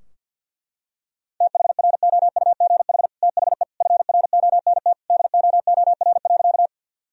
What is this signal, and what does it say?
Morse code, heard as a keyed tone, about 50 wpm, 700 Hz. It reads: THROUGH THE FRONT DOOR.
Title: Sentences from Top 1000 Words - Review Part 2 - 50wpm